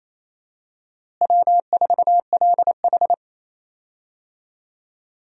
Morse code, heard as a keyed tone, about 28 wpm, 700 Hz. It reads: W4LH
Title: USA Call Signs - 4 Characters - Repeated 2x - 28wpm